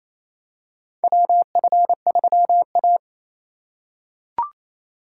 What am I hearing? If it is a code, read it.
Morse code, heard as a keyed tone, about 28 wpm, 700 Hz. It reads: WF3A
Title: USA Call Signs - 3x Character Spacing - 28wpm